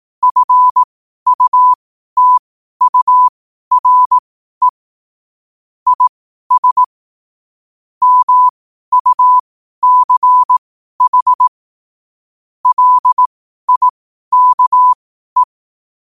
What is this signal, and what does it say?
Morse code, heard as a keyed tone, about 18 wpm, spaced as for 12 wpm, 1000 Hz. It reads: FUTURE IS MUCH LIKE
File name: qotdfiles / 12wpm